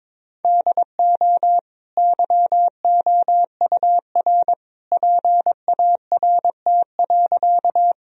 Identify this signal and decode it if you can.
Morse code, heard as a keyed tone, about 22 wpm, 700 Hz. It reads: DO YOUR PART.
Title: Sentences from Top 1000 Words - Review Part 1 - 22wpm